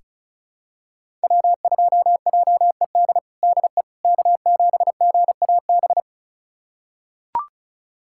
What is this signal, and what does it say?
Morse code, heard as a keyed tone, about 36 wpm, 700 Hz. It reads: W2JED DE K7GAB
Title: USA Callsign DE Callsign - 5 Characters - 35wpm